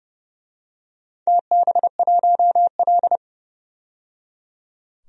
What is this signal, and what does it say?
Morse code, heard as a keyed tone, about 31 wpm, 700 Hz. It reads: TB1L
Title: DXCC-100 Call Signs - 4 Characters - 30wpm